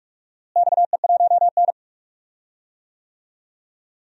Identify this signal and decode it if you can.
Morse code, heard as a keyed tone, about 44 wpm, 700 Hz. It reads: XE1N
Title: DXCC-100 Call Signs - 4 Characters - Repeated 2x - 45wpm